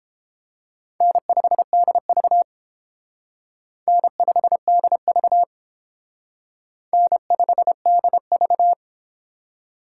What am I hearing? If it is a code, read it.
Morse code, heard as a keyed tone, about 30 wpm, 700 Hz. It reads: N5DV N5DV N5DV